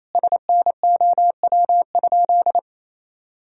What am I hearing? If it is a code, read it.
Morse code, heard as a keyed tone, about 28 wpm, 700 Hz. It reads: SNOW?
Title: Sentences from Top 1000 Words - Review Part 3 - 28wpm